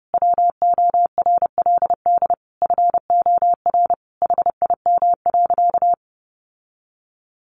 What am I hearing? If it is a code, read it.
Morse code, heard as a keyed tone, about 30 wpm, 700 Hz. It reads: WORLD FOR HIM.
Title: Sentences from Top 1300 Words 30wpm